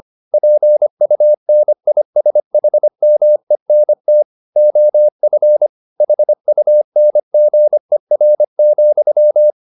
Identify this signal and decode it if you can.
Morse code, heard as a keyed tone, about 25 wpm, 600 Hz. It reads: PUNISHMENT OF HUNGER,